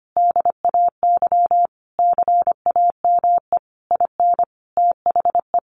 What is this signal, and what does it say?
Morse code, heard as a keyed tone, about 25 wpm, 700 Hz. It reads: DAY CAME IN THE
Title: Sentences from Top 1500 Words 25wpm